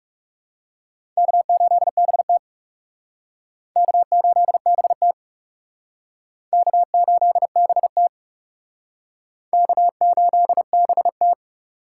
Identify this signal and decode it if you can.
Morse code, heard as a keyed tone, about 37 wpm, 700 Hz. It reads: K8BT K8BT K8BT K8BT
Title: Speed Racing - USA 1x2 Call Signs 30wpm